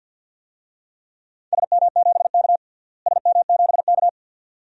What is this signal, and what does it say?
Morse code, heard as a keyed tone, about 51 wpm, 700 Hz. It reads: SM7K SM7K
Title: DXCC-100 Call Signs - 4 Characters - Repeated 2x - 50wpm